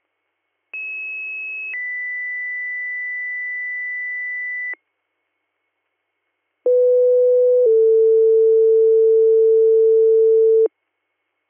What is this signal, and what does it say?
Two-tone page, heard as a one-second tone, then a three-second tone, about 2600 then 2000 Hz, about 500 then 450 Hz. It is heard from 0.7 to 10.7 s.